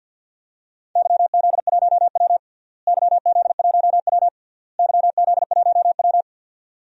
Morse code, heard as a keyed tone, about 52 wpm, 700 Hz. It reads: YZ1W YZ1W YZ1W
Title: DXCC-100 Call Signs - 4 Characters - Repeated 3x - 50wpm